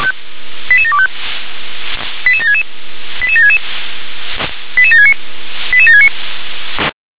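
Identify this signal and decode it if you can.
Five-tone selective call (ZVEI), heard as quick runs of five tones, tones of about 70 ms. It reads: 88725 88766 88766 88769 88769
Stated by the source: ZVEI five-tone call received on an SDR.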